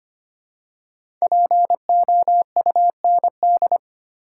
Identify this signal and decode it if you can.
Morse code, heard as a keyed tone, about 25 wpm, 700 Hz. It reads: POUND
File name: 5 Letter Words - First 3 Letters Given - 25wpm